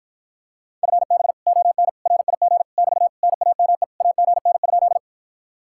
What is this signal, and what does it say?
Morse code, heard as a keyed tone, about 54 wpm, 700 Hz. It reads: FB ON RIG <BT> NAME AGN?